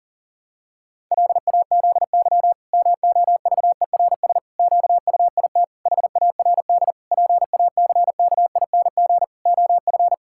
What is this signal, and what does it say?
Morse code, heard as a keyed tone, about 37 wpm, 700 Hz. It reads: LAZY MOVERS QUIT HARD PACKING OF
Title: Pangram Challenge v2 - 40wpm